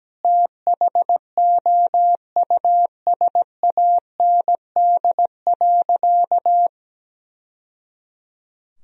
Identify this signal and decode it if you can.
Morse code, heard as a keyed tone, about 17 wpm, 700 Hz. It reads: THOUSAND.
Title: Sentences from Top 1000 Words - Review Part 3 - 17wpm